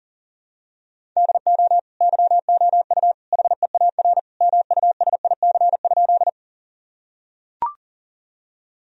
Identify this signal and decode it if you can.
Morse code, heard as a keyed tone, about 40 wpm, 700 Hz. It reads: DO YOU HEAR MUSIC?
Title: Sentences from Top 1000 Words - Review Part 1 - 40wpm